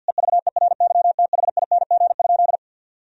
Morse code, heard as a keyed tone, about 52 wpm, 700 Hz. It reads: EVERYTHING?
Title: Sentences from Top 1000 Words - Review Part 2 - 50wpm